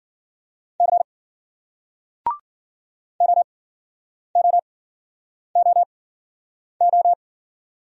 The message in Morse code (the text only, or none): C OOOO